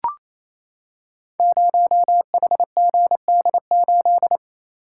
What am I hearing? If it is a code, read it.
Morse code, heard as a keyed tone, about 27 wpm, 700 Hz. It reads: 0HGD8
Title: Code Groups - 5 Letter:Numbers - 28wpm